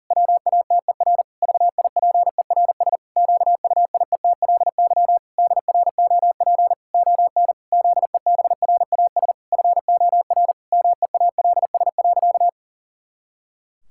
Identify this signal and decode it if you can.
Morse code, heard as a keyed tone, about 37 wpm, 700 Hz. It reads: WATER VIPERS QUIETLY DROP ON ZEBRAS FOR MEALS.
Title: Pangram Challenge v2 - 40wpm